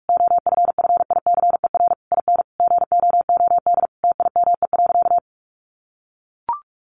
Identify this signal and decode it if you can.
Morse code, heard as a keyed tone, about 44 wpm, 700 Hz. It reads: OFFICER IN GOOD TIME.
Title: Sentences from Top 1200 Words 45wpm